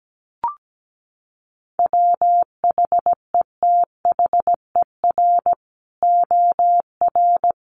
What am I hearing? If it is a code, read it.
Morse code, heard as a keyed tone, about 17 wpm, 700 Hz. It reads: WHETHER OR
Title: Sets of 3 Words 17wpm